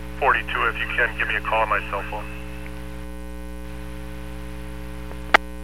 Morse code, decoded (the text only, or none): none